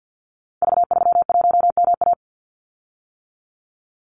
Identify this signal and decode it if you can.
Morse code, heard as a keyed tone, about 50 wpm, 700 Hz. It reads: V31MA
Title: Call Signs - 50wpm